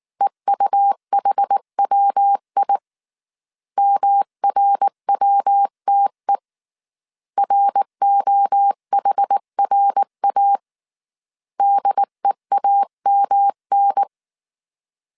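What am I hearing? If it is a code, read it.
Morse code, heard as a keyed tone, about 19 wpm, 800 Hz. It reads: EUHWI MRWTE ROHRA DEAMN
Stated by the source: Morse code dictation at 18 wpm from the IW3SGT QRS Morse code training course, five-character groups of 300 characters.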